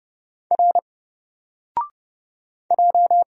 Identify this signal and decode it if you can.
Morse code, heard as a keyed tone, about 30 wpm, 700 Hz. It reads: R J
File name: Single-Letter-Number-30wpm